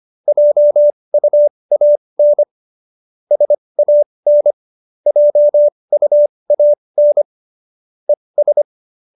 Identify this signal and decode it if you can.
Morse code, heard as a keyed tone, about 25 wpm, spaced as for 20 wpm, 590 Hz. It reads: JUAN SAN JUAN ES